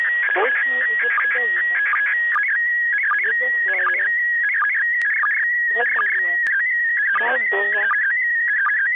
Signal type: SSTV (slow-scan television)